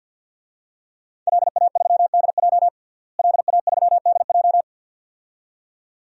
Morse code, heard as a keyed tone, about 53 wpm, 700 Hz. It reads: LA3DJ LA3DJ
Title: DXCC-100 Call Signs - 5 Characters - Repeated 2x - 50wpm